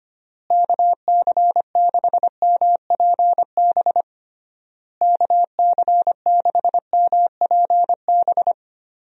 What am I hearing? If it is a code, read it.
Morse code, heard as a keyed tone, about 25 wpm, 700 Hz. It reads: KC6MPB KC6MPB
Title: USA Call Signs - 6 Characters - Repeated 3x - 25wpm